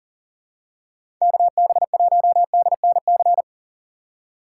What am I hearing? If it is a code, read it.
Morse code, heard as a keyed tone, about 40 wpm, 700 Hz. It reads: KB1DNC
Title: USA Call Signs - 6 Characters - 2x Character Spacing - 40wpm